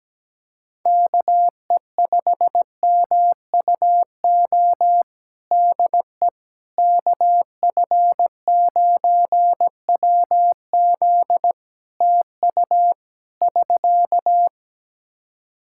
Morse code, heard as a keyed tone, about 17 wpm, 700 Hz. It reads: KE5MUO DE KF9WZ TU <SK>
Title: K1USN SST QSO Practice 17wpm